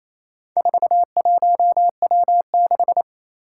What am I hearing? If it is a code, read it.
Morse code, heard as a keyed tone, about 28 wpm, 700 Hz. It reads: 41W6
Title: Four Challenging Letter-Number Combinations 28wpm